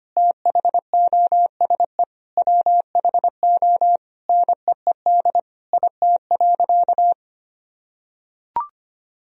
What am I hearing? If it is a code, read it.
Morse code, heard as a keyed tone, about 25 wpm, 700 Hz. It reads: THOSE WHO NEED IT.
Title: Sentences from Top 1000 Words - Review Part 4 - 25wpm